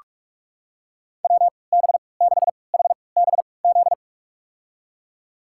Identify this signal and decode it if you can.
Morse code, heard as a keyed tone, about 45 wpm, spaced as for 25 wpm, 700 Hz. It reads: WB6HBZ
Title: USA Call Signs - 6 Characters - 3x Character Spacing - 45wpm